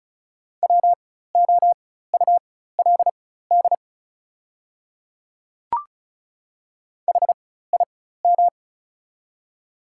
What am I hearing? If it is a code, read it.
Morse code, heard as a keyed tone, about 36 wpm, spaced as for 16 wpm, 690 Hz. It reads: WOULD HIM
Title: 4x Character Spacing - Top 100 Words - 35wpm